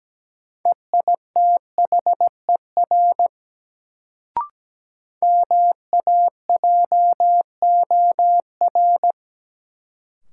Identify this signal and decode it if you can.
Morse code, heard as a keyed tone, about 17 wpm, 700 Hz. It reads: EITHER MAJOR
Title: Sentences from Top 600 Words - New Word Drill - 17wpm